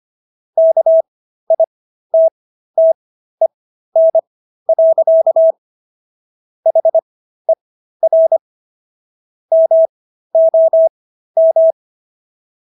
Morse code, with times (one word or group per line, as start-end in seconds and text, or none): 0.6-5.5 KITTEN.
6.7-8.4 HER
9.5-11.7 MOM